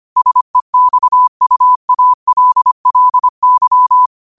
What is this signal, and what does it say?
Morse code, heard as a keyed tone, about 25 wpm, 1000 Hz. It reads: SEXUALLY
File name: qotdfiles / 25wpm